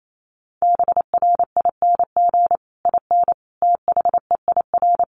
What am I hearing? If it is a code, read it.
Morse code, heard as a keyed tone, about 28 wpm, 700 Hz. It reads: BRING IN THEIR
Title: Sentences from Top 1300 Words 28wpm